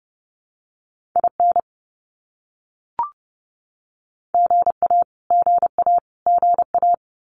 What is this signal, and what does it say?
Morse code, heard as a keyed tone, about 30 wpm, 700 Hz. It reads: IN GA GA GA
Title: US States Repeated 3x 30wpm